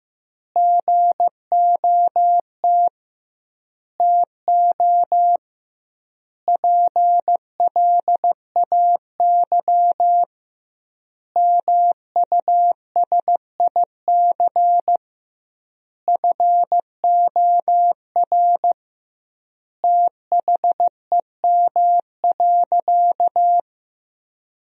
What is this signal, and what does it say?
Morse code, heard as a keyed tone, about 16 wpm, 700 Hz. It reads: GOT TO PLAY MUSIC FOR THEM.
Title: 2x Word-Spacing - Sentences from Top 300 Words - 15wpm